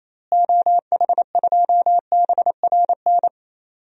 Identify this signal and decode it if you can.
Morse code, heard as a keyed tone, about 28 wpm, 700 Hz. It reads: OH2BRN
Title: DXCC-100 Call Signs - 6 Characters - Repeated 3x - 28wpm